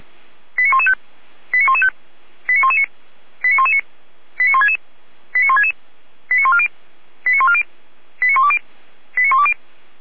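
Five-tone selective call (ZVEI), heard as quick runs of five tones, tones of about 70 ms. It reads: 89115 89115 89119 89119 89166 89166 89140 89140 89120 89120
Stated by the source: ZVEI five-tone selective call received on SDR.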